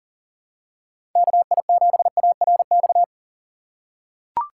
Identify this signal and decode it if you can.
Morse code, heard as a keyed tone, about 40 wpm, 700 Hz. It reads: KI7ARX
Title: USA Call Signs - 8x Character Spacing - 40wpm